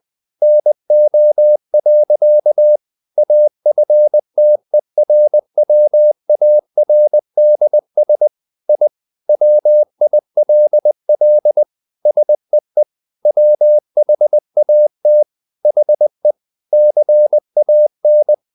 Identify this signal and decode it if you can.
Morse code, heard as a keyed tone, about 20 wpm, 600 Hz. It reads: NO. AFTERWARDS I WILL SEE WHAT HE CAN
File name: the old man and the sea part 2 20wpm